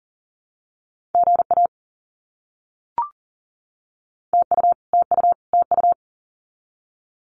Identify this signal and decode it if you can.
Morse code, heard as a keyed tone, about 40 wpm, 700 Hz. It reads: GA TU TU TU
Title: Abbreviations Repeated 3x - 40wpm